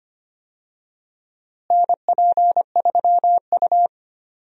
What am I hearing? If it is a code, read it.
Morse code, heard as a keyed tone, about 25 wpm, 700 Hz. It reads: NP3U
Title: USA Call Signs - 4 Characters - Repeated 2x - 25wpm